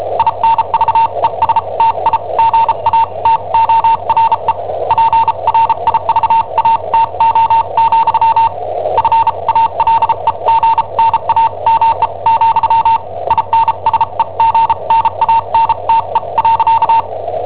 Morse code, heard as a keyed tone, about 34 wpm, spaced as for 29 wpm, 910 Hz. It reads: INVESTIGATORE PRIVATO, FALEGNAME, INSEGNANTE.
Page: Cassetta esercizi preparata da I7OHP per IW5ELP